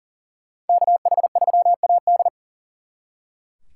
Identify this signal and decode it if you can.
Morse code, heard as a keyed tone, about 49 wpm, 700 Hz. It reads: KH3AD
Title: Speed Racing - USA 2x2 Call Signs 40wpm